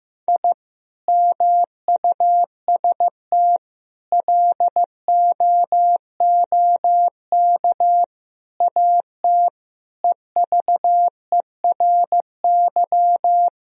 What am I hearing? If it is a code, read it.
Morse code, heard as a keyed tone, about 15 wpm, 700 Hz. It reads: I MUST LOOK AT EVERY
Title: Sentences from Top 1000 Words - Review Part 3 - 15wpm